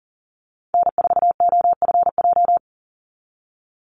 Morse code, heard as a keyed tone, about 40 wpm, 700 Hz. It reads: N4OFJ
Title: USA Call Signs 40wpm